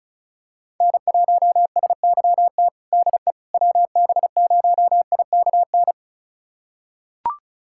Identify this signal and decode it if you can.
Morse code, heard as a keyed tone, about 35 wpm, 700 Hz. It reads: N1SYT DE WB0IKN
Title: USA Callsign DE Callsign 35wpm